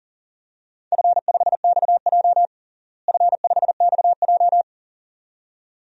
Morse code, heard as a keyed tone, about 40 wpm, 700 Hz. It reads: F5XJ F5XJ
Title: DXCC-100 Call Signs - 4 Characters - Repeated 2x - 40wpm